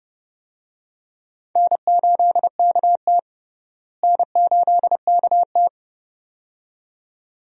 Morse code, heard as a keyed tone, about 30 wpm, 700 Hz. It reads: N8KT N8KT
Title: USA Call Signs - 4 Characters - Repeated 2x - 30wpm